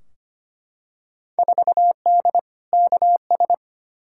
Morse code, heard as a keyed tone, about 25 wpm, 700 Hz. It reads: 4D KS
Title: Field Day Exchanges Encore - 25wpm